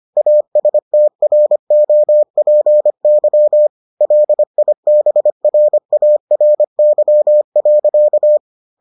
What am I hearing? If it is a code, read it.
Morse code, heard as a keyed tone, about 25 wpm, 600 Hz. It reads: ASTROPY LIBRARY.